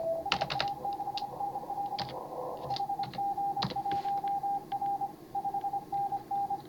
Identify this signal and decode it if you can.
Morse code, heard as a keyed tone, about 28 wpm, 810 Hz, with noise. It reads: SA6NL HG1R 5NN